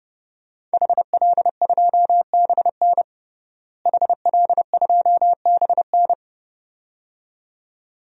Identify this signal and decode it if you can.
Morse code, heard as a keyed tone, about 30 wpm, 700 Hz. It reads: HL2BN HL2BN
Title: DXCC-100 Call Signs - 5 Characters - Repeated 2x - 30wpm